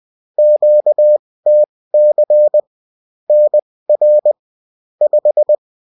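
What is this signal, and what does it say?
Morse code, heard as a keyed tone, about 20 wpm, spaced as for 16 wpm, 600 Hz. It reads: QTC NR 5
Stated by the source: UFT bulletin sent in CW.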